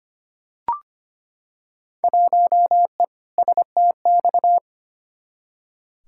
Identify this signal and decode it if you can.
Morse code, heard as a keyed tone, about 25 wpm, 700 Hz. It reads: 1E STX
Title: Field Day Exchanges - 25wpm